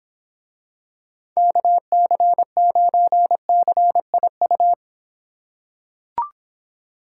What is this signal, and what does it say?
Morse code, heard as a keyed tone, about 25 wpm, 700 Hz. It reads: KC9CIU